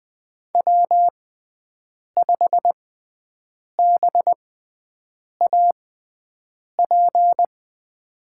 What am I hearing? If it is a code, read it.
Morse code, heard as a keyed tone, about 20 wpm, spaced as for 7 wpm, 700 Hz. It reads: W5BAP
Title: USA Call Signs - 5 Characters - 6x Character Spacing - 20wpm